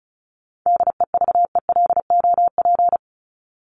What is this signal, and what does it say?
Morse code, heard as a keyed tone, about 34 wpm, 700 Hz. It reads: DEVELOP
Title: ED Suffix - Top 100 Words with Base Words 35wpm